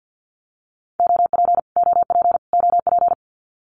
Morse code, heard as a keyed tone, about 50 wpm, 700 Hz. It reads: OP OP OP
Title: Abbreviations Repeated 3x - 50wpm